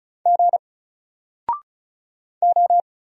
Morse code, heard as a keyed tone, about 34 wpm, 700 Hz. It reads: G O
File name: Single-Letter-Number-35wpm